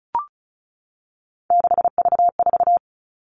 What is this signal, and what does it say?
Morse code, heard as a keyed tone, about 36 wpm, 700 Hz. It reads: 6V4
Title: Three Challenging Letter-Number Combinations 35wpm